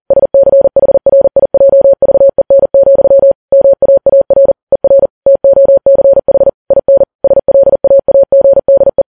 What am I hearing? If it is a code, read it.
Morse code, heard as a keyed tone, about 40 wpm, 550 Hz. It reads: SCHRIJVEN, MAAR ER TOCH IN SLAAGDE
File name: Files Web tls / Morse on9fv / morse40kps